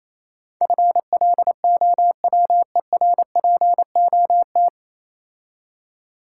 Morse code, heard as a keyed tone, about 27 wpm, 700 Hz. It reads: FLOWERPOT
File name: Easy Words - Speed-Racing with Hints - 22wpm